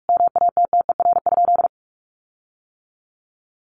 Morse code, heard as a keyed tone, about 44 wpm, 700 Hz. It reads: MATTER?
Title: Sentences from Top 1300 Words 45wpm